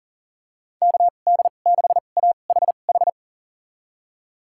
Morse code, heard as a keyed tone, about 37 wpm, spaced as for 28 wpm, 700 Hz. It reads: KD6AHH
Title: USA Call Signs - 6 Characters - 2x Character Spacing - 40wpm